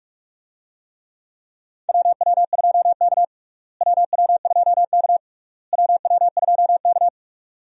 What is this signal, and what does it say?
Morse code, heard as a keyed tone, about 41 wpm, 700 Hz. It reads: WW2K WW2K WW2K
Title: USA Call Signs - 4 Characters - Repeated 3x - 45wpm